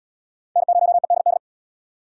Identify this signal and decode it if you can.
Morse code, heard as a keyed tone, about 77 wpm, 700 Hz. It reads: N0EDD